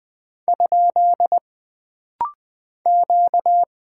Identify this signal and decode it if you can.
Morse code, heard as a keyed tone, about 20 wpm, 700 Hz. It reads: ? Q
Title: Single Character (Letter:Number:Punctuation:Prosign) - Rapid-Fire - 20wpm